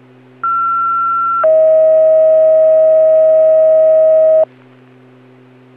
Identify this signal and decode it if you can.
Two-tone page, heard as a one-second tone, then a three-second tone, about 1360 then 640 Hz. It is heard from 0.4 to 4.4 s.